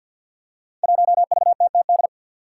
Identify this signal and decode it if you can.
Morse code, heard as a keyed tone, about 47 wpm, 700 Hz. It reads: 1VTTD
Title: Five Challenging Letter-Number Combinations 50wpm